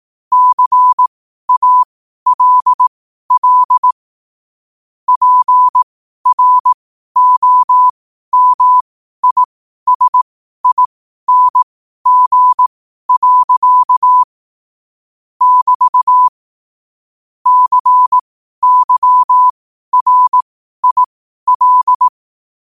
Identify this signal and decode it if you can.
Morse code, heard as a keyed tone, about 18 wpm, spaced as for 12 wpm, 1000 Hz. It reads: CALL PROMISING. <BT> CYRIL
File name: qotdfiles / 12wpm